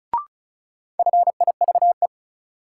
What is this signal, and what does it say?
Morse code, heard as a keyed tone, about 36 wpm, 700 Hz. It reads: FIVE
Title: Sets of 5 Words - New Word Drill - 35wpm